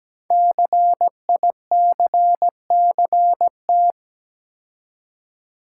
Morse code, heard as a keyed tone, about 17 wpm, 700 Hz. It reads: CICCT
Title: Code Groups - 5 Letters - 17wpm